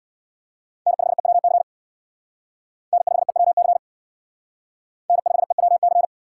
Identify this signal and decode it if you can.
Morse code, heard as a keyed tone, about 59 wpm, 700 Hz. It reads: N5EWC N5EWC N5EWC
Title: Speed Racing - USA 1x3 Call Signs 50wpm